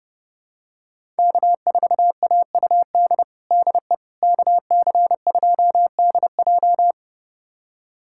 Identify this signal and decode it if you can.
Morse code, heard as a keyed tone, about 30 wpm, 700 Hz. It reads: K4AUD DE KC2DJ
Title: USA Callsign DE Callsign - 5 Characters - 30wpm